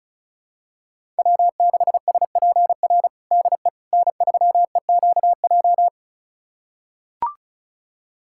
Morse code, heard as a keyed tone, about 34 wpm, 700 Hz. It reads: W6SPR DE N3EQJ
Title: USA Callsign DE Callsign - 5 Characters - 35wpm